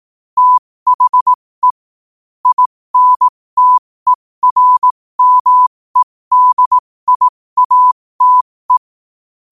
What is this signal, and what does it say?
Morse code, heard as a keyed tone, about 18 wpm, spaced as for 15 wpm, 1000 Hz. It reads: THE INTERMEDIATE
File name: qotdfiles / 15wpm